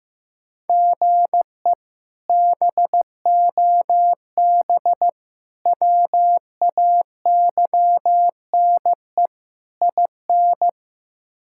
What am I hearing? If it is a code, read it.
Morse code, heard as a keyed tone, about 16 wpm, 700 Hz. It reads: GE BOB WAYNE IN
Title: K1USN SST QSO Practice - 15wpm